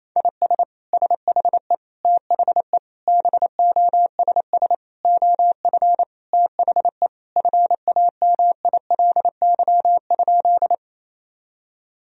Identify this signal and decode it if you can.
Morse code, heard as a keyed tone, about 28 wpm, 700 Hz. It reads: IS SHE THE BOSS OF THE FAMILY?
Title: Sentences from Top 1000 Words - Review Part 4 - 28wpm